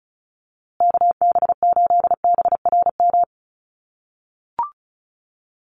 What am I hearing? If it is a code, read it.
Morse code, heard as a keyed tone, about 34 wpm, 700 Hz. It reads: KB8BRM
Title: USA Call Signs Repeated 2x 35wpm